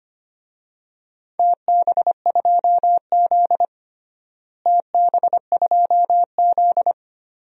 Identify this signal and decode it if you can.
Morse code, heard as a keyed tone, about 25 wpm, 700 Hz. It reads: TB2Z TB2Z
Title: DXCC-100 Call Signs - 4 Characters - Repeated 2x - 25wpm